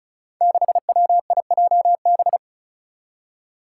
Morse code, heard as a keyed tone, about 36 wpm, 700 Hz. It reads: 6WIJB
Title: Five Challenging Letter-Number Combinations 35wpm